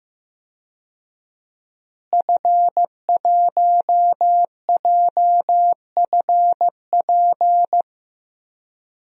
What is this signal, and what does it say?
Morse code, heard as a keyed tone, about 15 wpm, 700 Hz. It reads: F1JFP
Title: DXCC-100 Call Signs - 5 Characters - Repeated 2x - 15wpm